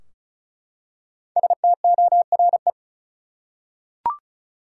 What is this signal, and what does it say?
Morse code, heard as a keyed tone, about 34 wpm, 700 Hz. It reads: STORE
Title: Five Letter Words - 35wpm